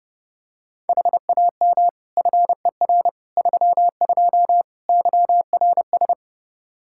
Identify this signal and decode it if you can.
Morse code, heard as a keyed tone, about 32 wpm, 700 Hz. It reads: HAM FER 32 YRS
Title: Practice QSO - Exchange 3 - More Info - 30wpm